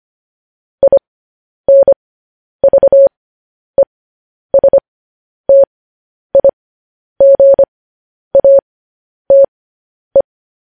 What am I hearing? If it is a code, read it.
Morse code, heard as a keyed tone, about 25 wpm, spaced as for 10 wpm, 550 Hz. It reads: INVESTIGATE
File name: podcast / News headlines-1x-25x10